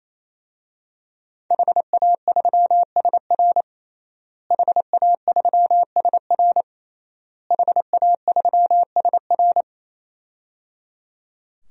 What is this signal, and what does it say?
Morse code, heard as a keyed tone, about 27 wpm, 700 Hz. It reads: HA3SR HA3SR HA3SR
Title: DXCC-100 Call Signs - 5 Characters - Repeated 3x - 28wpm